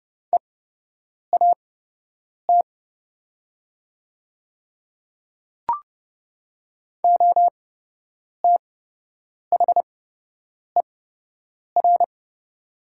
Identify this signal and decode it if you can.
Morse code, heard as a keyed tone, about 30 wpm, spaced as for 8 wpm, 700 Hz. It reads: EAT OTHER
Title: 8x Character Spacing - Top 300 Words - 30wpm